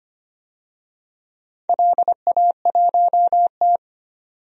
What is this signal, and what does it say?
Morse code, heard as a keyed tone, about 25 wpm, 700 Hz. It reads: LA1T
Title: DXCC-100 Call Signs - 4 Characters - 25wpm